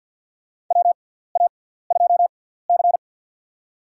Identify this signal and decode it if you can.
Morse code, heard as a keyed tone, about 53 wpm, spaced as for 17 wpm, 700 Hz. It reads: WA2C
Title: USA Call Signs - 4 Characters - 6x Character Spacing - 50wpm